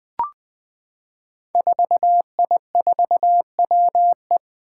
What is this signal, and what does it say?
Morse code, heard as a keyed tone, about 20 wpm, 700 Hz. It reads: 4I4WE
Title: Five Challenging Letter-Number Combinations 20wpm